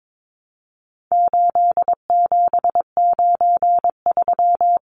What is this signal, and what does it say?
Morse code, heard as a keyed tone, about 22 wpm, 700 Hz. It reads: 8793
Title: Numbers - 4-Digit 22wpm